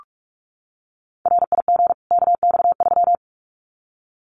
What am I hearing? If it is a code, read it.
Morse code, heard as a keyed tone, about 44 wpm, 700 Hz. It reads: RIG KX3
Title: Common QSO Phrases 45wpm